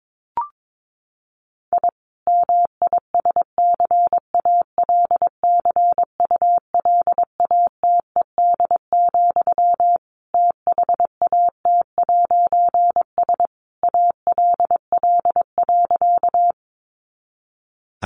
Morse code, heard as a keyed tone, about 22 wpm, 700 Hz. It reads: I MISCALCULATED, THAT'S ALL.
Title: A Sound Of Thunder 22wpm